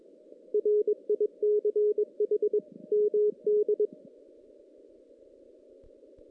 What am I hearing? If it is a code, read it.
Morse code, heard as a keyed tone, about 22 wpm, 410 Hz. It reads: RICH MD
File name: 2016 JAN CW NAQP / 2016 Jan naqp k5ot - 0-5